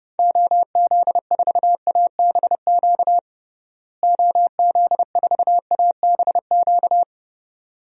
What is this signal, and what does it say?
Morse code, heard as a keyed tone, about 32 wpm, 700 Hz. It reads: OZ4ABQ OZ4ABQ
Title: DXCC-100 Call Signs - 6 Characters - Repeated 2x - 30wpm